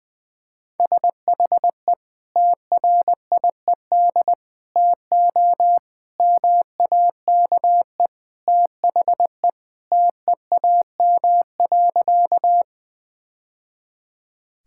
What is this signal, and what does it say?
Morse code, heard as a keyed tone, about 21 wpm, 700 Hz. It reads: SHE TRIED TO MAKE THE TEAM.
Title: Sentences with 3-Word Phrases - 20wpm